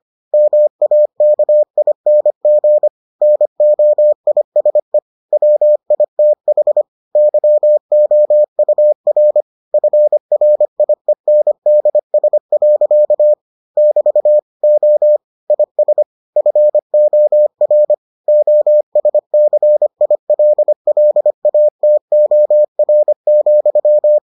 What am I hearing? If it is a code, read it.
Morse code, heard as a keyed tone, about 25 wpm, 600 Hz. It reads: MAKING NOISE WITH YOUR FRIENDS. <BT> O IS FOR OSCILLATOR,